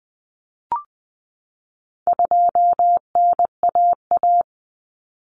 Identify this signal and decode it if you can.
Morse code, heard as a keyed tone, about 20 wpm, 700 Hz. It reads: 2NAA
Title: Common Cut Numbers - 4-Digit 20wpm